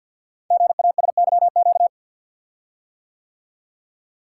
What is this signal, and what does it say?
Morse code, heard as a keyed tone, about 50 wpm, 700 Hz. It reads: GASYQ
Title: Code Groups - 5 Letter:Numbers - 50wpm